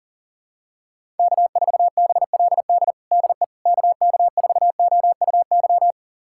Morse code, heard as a keyed tone, about 40 wpm, 700 Hz. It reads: K4BLD DE KK4OUY
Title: USA Callsign DE Callsign 40wpm